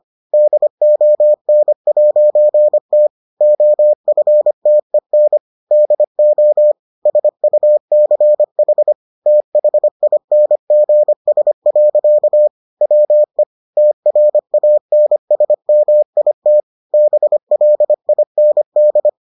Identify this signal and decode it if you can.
Morse code, heard as a keyed tone, about 25 wpm, 600 Hz. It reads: DON'T OFTEN DO SUCH THINGS. WE TRANSMIT BLIND